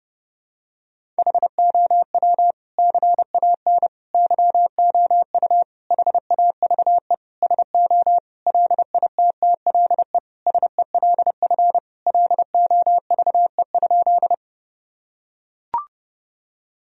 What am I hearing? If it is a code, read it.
Morse code, heard as a keyed tone, about 32 wpm, 700 Hz. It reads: HOW CAN YOU HAVE SO LITTLE SELF LOVE?
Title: Sentences from Top 1000 Words - Review Part 1 - 30wpm